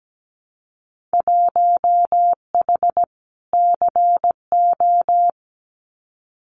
Morse code, heard as a keyed tone, about 17 wpm, 700 Hz. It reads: 1H CO
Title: Winter Field Day Exchanges 17wpm